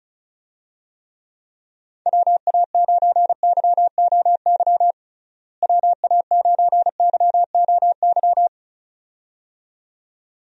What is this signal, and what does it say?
Morse code, heard as a keyed tone, about 36 wpm, 700 Hz. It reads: WA9YOY WA9YOY
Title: USA Call Signs - 6 Characters - Repeated 2x - 35wpm